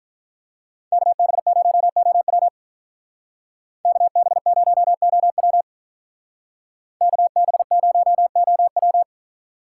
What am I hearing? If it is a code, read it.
Morse code, heard as a keyed tone, about 44 wpm, 700 Hz. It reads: KB0OW KB0OW KB0OW